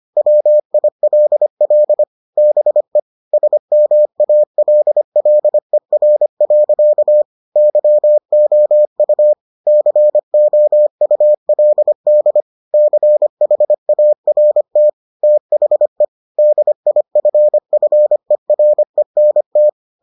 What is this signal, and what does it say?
Morse code, heard as a keyed tone, about 25 wpm, 600 Hz. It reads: WILL BE SMALLER. YOU COULD CHART THE DIFFERENT